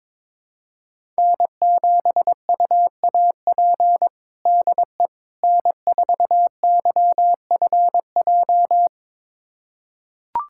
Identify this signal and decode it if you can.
Morse code, heard as a keyed tone, about 21 wpm, 700 Hz. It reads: N7UAP DE N4YFJ
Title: USA Callsign DE Callsign 22wpm